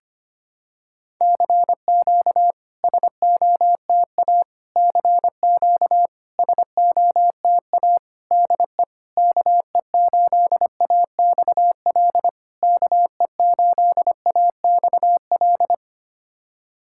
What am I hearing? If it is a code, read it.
Morse code, heard as a keyed tone, about 25 wpm, 690 Hz. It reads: CQ SOTA CQ SOTA DE KE8AXL KE8AXL
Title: SOTA QSO Practice 25wpm